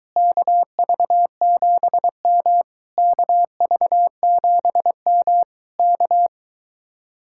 Morse code, heard as a keyed tone, about 22 wpm, 700 Hz. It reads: KV7M KV7M K